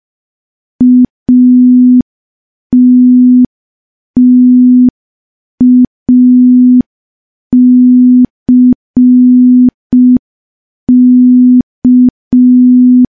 Morse code, heard as a keyed tone, about 5 wpm, 250 Hz. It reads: ATTACK
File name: morse / morse-training-05WPM